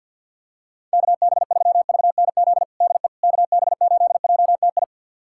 Morse code, heard as a keyed tone, about 51 wpm, 700 Hz. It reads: KB3VNZ DE KB8JTI
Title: USA Callsign DE Callsign 50wpm